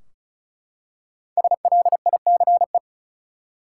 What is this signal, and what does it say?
Morse code, heard as a keyed tone, about 36 wpm, 700 Hz. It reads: SLICE
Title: Five Letter Words - 35wpm